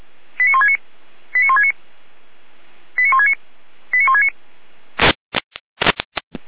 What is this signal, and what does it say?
Five-tone selective call (ZVEI), heard as quick runs of five tones, tones of about 70 ms. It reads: 89169 89169 89169 89169